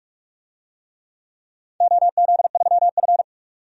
Morse code, heard as a keyed tone, about 46 wpm, 700 Hz. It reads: OZ3F
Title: DXCC-100 Call Signs - 4 Characters - 45wpm